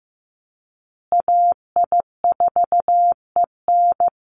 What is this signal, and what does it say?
Morse code, heard as a keyed tone, about 15 wpm, 700 Hz. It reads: AI4EN
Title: USA Call Signs Repeated 2x 15wpm